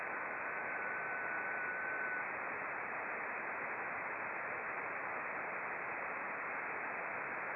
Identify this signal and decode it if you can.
FT8.
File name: contests / sept-vhf-k5tr- -0300